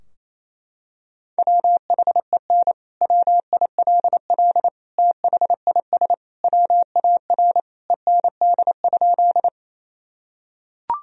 Morse code, heard as a keyed tone, about 28 wpm, 700 Hz. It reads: WHEN WILL THIS WAR END?
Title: Sentences from Top 1000 Words - Review Part 2 - 28wpm